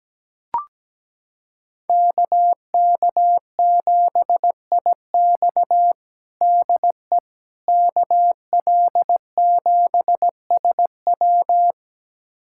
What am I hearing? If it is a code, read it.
Morse code, heard as a keyed tone, about 17 wpm, 700 Hz. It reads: KK7IX DE KL7SW
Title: USA Callsign DE Callsign - 5 Characters - 17wpm